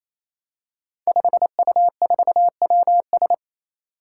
Morse code, heard as a keyed tone, about 27 wpm, 700 Hz. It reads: 5U4WS
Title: Five Challenging Letter-Number Combinations 28wpm